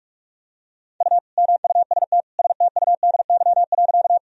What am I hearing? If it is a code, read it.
Morse code, heard as a keyed tone, about 46 wpm, 700 Hz. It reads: U MUST STUDY.
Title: Sentences from Top 300 Words and Abbreviations 45wpm